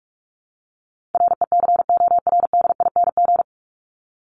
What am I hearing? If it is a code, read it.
Morse code, heard as a keyed tone, about 44 wpm, 700 Hz. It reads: RECORDING
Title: Top 1501-1750 Words 45wpm